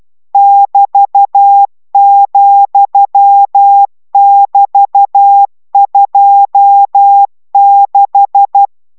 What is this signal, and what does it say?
Morse code, heard as a keyed tone, about 12 wpm, 800 Hz. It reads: <BT>,<BT>26